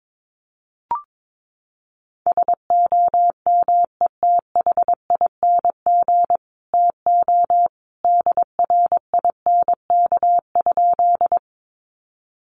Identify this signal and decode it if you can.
Morse code, heard as a keyed tone, about 22 wpm, 700 Hz. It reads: SOMETHING TO DRINK?
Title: Sentences from Top 1500 Words 22wpm